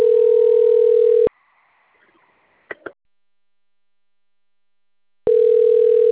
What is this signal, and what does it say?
Ringback tone (North America), 440 + 480 Hz.